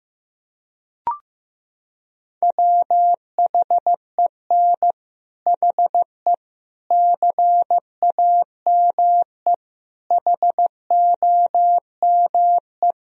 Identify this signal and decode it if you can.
Morse code, heard as a keyed tone, about 15 wpm, 700 Hz. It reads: WHEN HE CAME HOME
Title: Sentences from Top 1000 Words - Review Part 3 - 15wpm